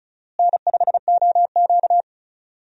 Morse code, heard as a keyed tone, about 36 wpm, 700 Hz. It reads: N5OQ
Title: USA Call Signs - 4 Characters - Repeated 2x - 35wpm